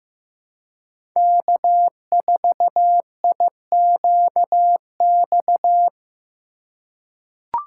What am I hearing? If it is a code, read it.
Morse code, heard as a keyed tone, about 15 wpm, 700 Hz. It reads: K4IQX
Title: USA Call Signs - 5 Characters - 4x Character Spacing - 15wpm